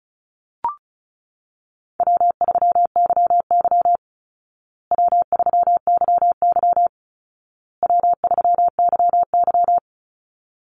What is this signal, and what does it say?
Morse code, heard as a keyed tone, about 34 wpm, 700 Hz. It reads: W3YY W3YY W3YY
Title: Easy Call Signs Repeated 3x - 35wpm